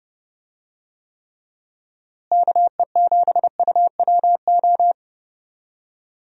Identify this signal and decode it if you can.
Morse code, heard as a keyed tone, about 30 wpm, 700 Hz. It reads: KE7UWO
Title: USA Call Signs - 6 Characters - Repeated 3x - 30wpm